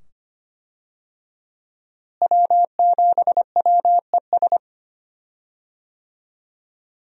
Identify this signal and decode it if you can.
Morse code, heard as a keyed tone, about 25 wpm, 700 Hz. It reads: W7WES
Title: USA Call Signs - 5 Characters - Repeated 2x - 25wpm